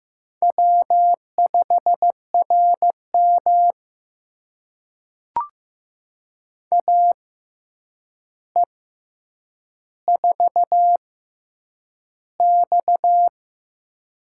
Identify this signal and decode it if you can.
Morse code, heard as a keyed tone, about 15 wpm, 700 Hz. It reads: W 5 R M A E 4 X
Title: USA Call Signs - 4 Characters - 6x Character Spacing - 15wpm